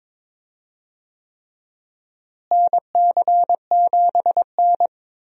Morse code, heard as a keyed tone, about 22 wpm, 700 Hz. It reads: NC7N
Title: USA Call Signs - 4 Characters - Repeated 3x - 22wpm